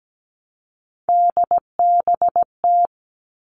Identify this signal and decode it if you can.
Morse code, heard as a keyed tone, about 17 wpm, 700 Hz. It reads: DBT
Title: All Cut Numbers - 3-Digit 17wpm